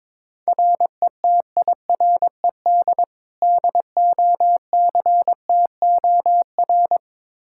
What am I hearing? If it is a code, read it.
Morse code, heard as a keyed tone, about 22 wpm, 700 Hz. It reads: RETIRED DOCTOR
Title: Practice QSO - Exchange 3 - More Info - 22wpm